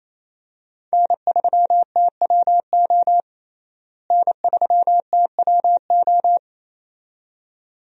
Morse code, heard as a keyed tone, about 28 wpm, 700 Hz. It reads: N3TWO N3TWO
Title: USA Call Signs - 5 Characters - Repeated 2x - 28wpm